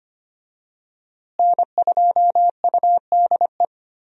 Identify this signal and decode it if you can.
Morse code, heard as a keyed tone, about 25 wpm, 700 Hz. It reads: N2UDE
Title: USA Call Signs - 5 Characters - Repeated 3x - 25wpm